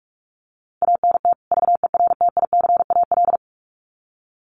Morse code, heard as a keyed tone, about 44 wpm, 700 Hz. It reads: ANT VERTICAL
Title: Common QSO Phrases 45wpm